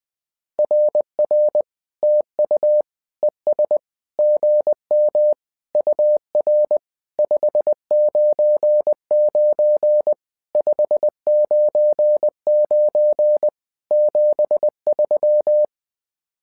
Morse code, heard as a keyed tone, about 20 wpm, 600 Hz. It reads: RR TU ES GM UR 599 599 73
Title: SOTA QSO Practice 20wpm